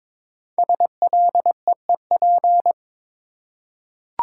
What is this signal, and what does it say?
Morse code, heard as a keyed tone, about 22 wpm, 700 Hz. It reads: SLEEP
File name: 5 Letter Words - First 3 Letters Given - 22wpm